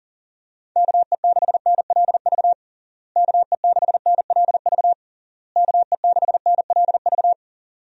Morse code, heard as a keyed tone, about 40 wpm, 700 Hz. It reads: KE6NLV KE6NLV KE6NLV
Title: USA Call Signs - 6 Characters - Repeated 3x - 40wpm